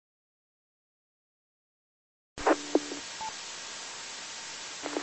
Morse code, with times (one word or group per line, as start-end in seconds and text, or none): none